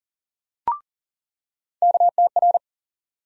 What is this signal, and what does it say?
Morse code, heard as a keyed tone, about 40 wpm, 700 Hz. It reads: KTR
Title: USA Call Sign Postfixes - 3 Letters 40wpm